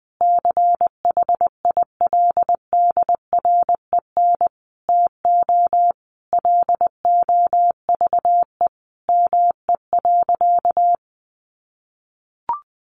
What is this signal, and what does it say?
Morse code, heard as a keyed tone, about 20 wpm, 700 Hz. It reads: CHILDREN TO LOVE ME.
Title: Sentences from Top 400 Words 20wpm